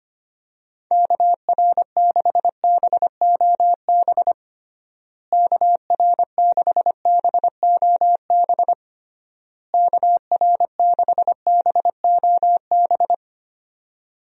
Morse code, heard as a keyed tone, about 25 wpm, 700 Hz. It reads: KR6BOB KR6BOB KR6BOB
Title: USA Call Signs - 6 Characters - Repeated 3x - 25wpm